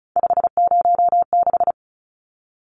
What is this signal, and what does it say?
Morse code, heard as a keyed tone, about 34 wpm, 700 Hz. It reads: 506
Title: Numbers - 3-Digit 35wpm